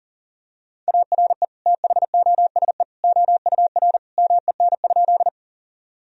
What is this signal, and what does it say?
Morse code, heard as a keyed tone, about 40 wpm, 700 Hz. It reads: ARE THOSE OUR MEN?
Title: Sentences from Top 1000 Words - Review Part 1 - 40wpm